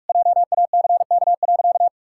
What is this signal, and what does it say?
Morse code, heard as a keyed tone, about 44 wpm, 700 Hz. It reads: JACK.
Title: Pangram Challenge v2 - 45wpm